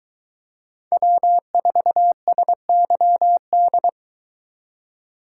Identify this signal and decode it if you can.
Morse code, heard as a keyed tone, about 23 wpm, 700 Hz. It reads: W4SYD